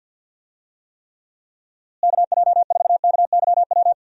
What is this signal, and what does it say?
Morse code, heard as a keyed tone, about 52 wpm, 700 Hz. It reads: KJ4KYW
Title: USA Call Signs - 6 Characters - Repeated 3x - 50wpm